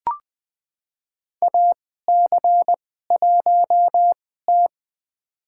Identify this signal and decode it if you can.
Morse code, heard as a keyed tone, about 20 wpm, spaced as for 14 wpm, 700 Hz. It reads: AC1T
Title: USA Call Signs - 2x Character Spacing - 20wpm